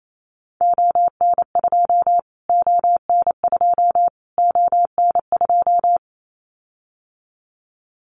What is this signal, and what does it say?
Morse code, heard as a keyed tone, about 28 wpm, 700 Hz. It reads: ON2 ON2 ON2
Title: Callsign Prefixes for Top 25 DXCC - Repeated 3x - 28wpm